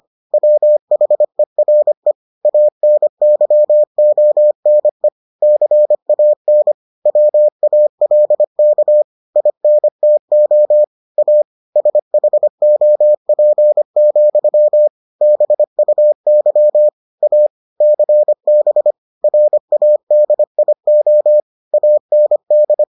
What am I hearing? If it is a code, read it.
Morse code, heard as a keyed tone, about 25 wpm, 600 Hz. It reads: WHERE ANYONE CAN WALK INTO A SHOP, BUY A CB RADIO AND